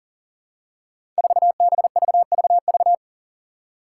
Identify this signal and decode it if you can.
Morse code, heard as a keyed tone, about 37 wpm, 700 Hz. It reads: 4BVVV
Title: Five Challenging Letter-Number Combinations 40wpm